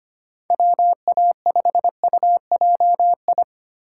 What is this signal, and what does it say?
Morse code, heard as a keyed tone, about 25 wpm, 700 Hz. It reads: WA5UJI
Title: USA Call Signs - 6 Characters - 3x Character Spacing - 25wpm